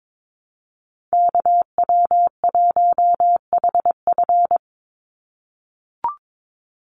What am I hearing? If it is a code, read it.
Morse code, heard as a keyed tone, about 22 wpm, 700 Hz. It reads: KW1HF
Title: USA Call Signs 22wpm